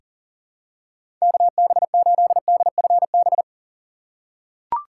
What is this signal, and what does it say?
Morse code, heard as a keyed tone, about 38 wpm, 700 Hz. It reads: KB8DFB
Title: USA Call Signs - 2x Character Spacing - 40wpm